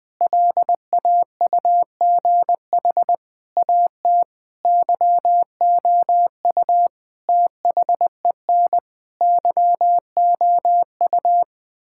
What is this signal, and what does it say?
Morse code, heard as a keyed tone, about 20 wpm, 700 Hz. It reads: LAUGH AT YOU THEN YOU
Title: Sentences with 8-Word Phrases - 20wpm